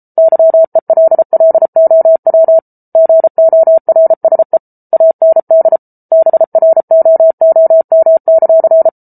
Morse code, heard as a keyed tone, about 34 wpm, 650 Hz. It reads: YELLOW GORSE AND BROOM;